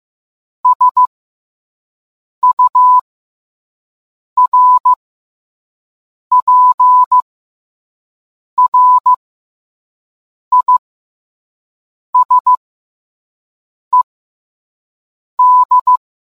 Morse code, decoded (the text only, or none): SURPRISED